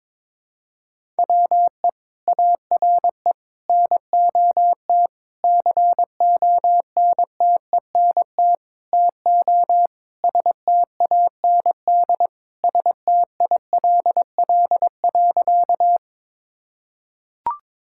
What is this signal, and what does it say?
Morse code, heard as a keyed tone, about 22 wpm, 700 Hz. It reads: WE ARE NOT CONTENT TO STAND STILL.
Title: Sentences from Top 1000 Words - Review Part 6 - 22wpm